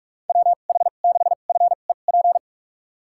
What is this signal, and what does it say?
Morse code, heard as a keyed tone, about 46 wpm, spaced as for 32 wpm, 700 Hz. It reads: WH6FEP
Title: USA Call Signs - 6 Characters - 2x Character Spacing - 45wpm